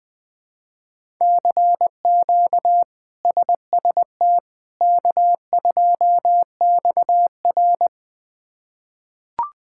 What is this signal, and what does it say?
Morse code, heard as a keyed tone, about 20 wpm, 700 Hz. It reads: CQ SST K2XR
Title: K1USN SST QSO Practice - 20wpm